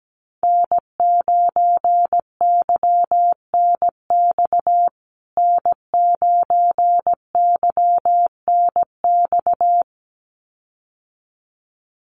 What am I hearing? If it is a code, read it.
Morse code, heard as a keyed tone, about 17 wpm, 700 Hz. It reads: N9YNX N9YNX
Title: USA Call Signs Repeated 2x 17wpm